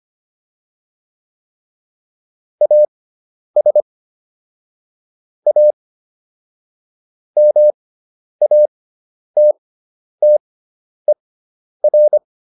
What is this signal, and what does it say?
Morse code, heard as a keyed tone, about 25 wpm, spaced as for 10 wpm, 600 Hz. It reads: AS A MATTER
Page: intermediate-practice-files